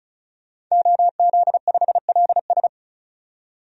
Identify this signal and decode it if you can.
Morse code, heard as a keyed tone, about 35 wpm, 700 Hz. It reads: OZ5LS
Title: DXCC-100 Call Signs - 5 Characters - 35wpm